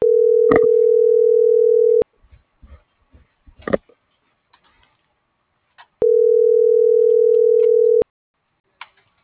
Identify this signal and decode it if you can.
Ringback tone (North America), 440 + 480 Hz.